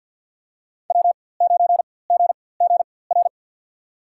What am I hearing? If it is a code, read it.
Morse code, heard as a keyed tone, about 49 wpm, spaced as for 23 wpm, 700 Hz. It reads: W9GGR
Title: USA Call Signs - 5 Characters - 4x Character Spacing - 50wpm